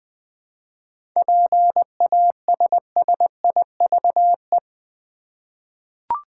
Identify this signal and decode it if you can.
Morse code, heard as a keyed tone, about 20 wpm, 700 Hz. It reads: PASSIVE
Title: 7 Letter Words - First 3 Letters Given - 20wpm